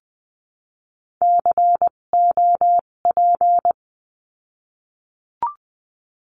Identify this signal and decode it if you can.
Morse code, heard as a keyed tone, about 20 wpm, spaced as for 17 wpm, 700 Hz. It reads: COP
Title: 257 Lesson 42b - Words 14-20wpm